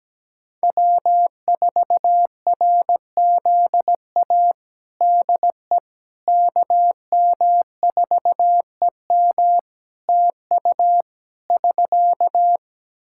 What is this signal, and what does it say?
Morse code, heard as a keyed tone, about 17 wpm, 700 Hz. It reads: W4RZA DE KM4EM TU <SK>
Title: Practice QSO - Exchange 4 - Ending - 17wpm